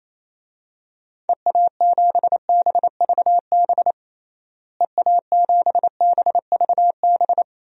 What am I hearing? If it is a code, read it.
Morse code, heard as a keyed tone, about 27 wpm, 700 Hz. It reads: EA7BVB EA7BVB
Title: DXCC-100 Call Signs - 6 Characters - Repeated 2x - 28wpm